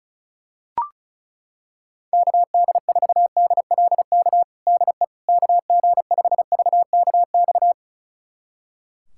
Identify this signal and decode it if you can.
Morse code, heard as a keyed tone, about 36 wpm, 700 Hz. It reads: KD4DLK DE KG5VKX
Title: USA Callsign DE Callsign - 6 Characters - 35wpm